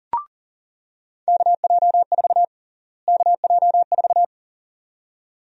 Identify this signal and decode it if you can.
Morse code, heard as a keyed tone, about 38 wpm, 700 Hz. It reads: KJ4 KJ4
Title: USA Call Sign Prefixes - Repeated 2x - 40wpm